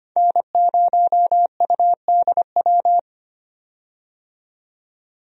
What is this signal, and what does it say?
Morse code, heard as a keyed tone, about 25 wpm, 700 Hz. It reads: N0UDW
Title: USA Call Signs - 5 Characters - 25wpm